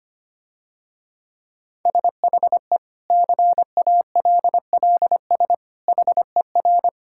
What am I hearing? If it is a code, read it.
Morse code, heard as a keyed tone, about 25 wpm, 700 Hz. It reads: SHE CALLS HER
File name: practice / 25wpm / Difficult Sentences with Hints - 25wpm